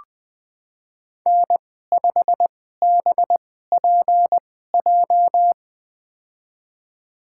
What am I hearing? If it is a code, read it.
Morse code, heard as a keyed tone, about 20 wpm, spaced as for 14 wpm, 700 Hz. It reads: N5BPJ
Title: USA Call Signs - 5 Characters - 2x Character Spacing - 20wpm